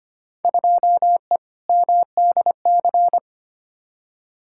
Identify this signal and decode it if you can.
Morse code, heard as a keyed tone, about 25 wpm, 700 Hz. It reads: 2E MDC
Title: Field Day Exchanges - 25wpm